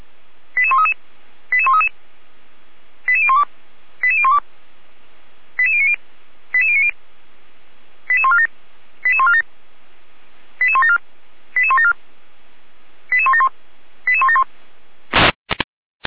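Five-tone selective call (ZVEI), heard as quick runs of five tones, tones of about 70 ms. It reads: 88133 88133 88012 88012 88090 88090 88147 88147 88174 88174 88171 88171
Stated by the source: ZVEI five-tone selective call received with an SDR.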